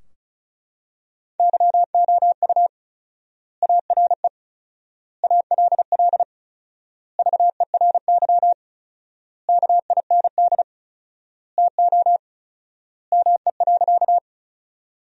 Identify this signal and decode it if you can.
Morse code, heard as a keyed tone, about 36 wpm, 700 Hz. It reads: YOU ARE ALL VERY KIND TO ME.
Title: 4x Word-Spacing - Sentences from Top 200 Words - 35wpm